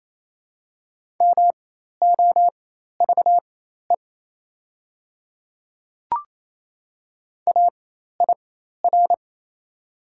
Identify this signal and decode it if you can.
Morse code, heard as a keyed tone, about 28 wpm, spaced as for 13 wpm, 700 Hz. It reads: MOVE AIR
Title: 4x Character Spacing - Top 200 Words - 28wpm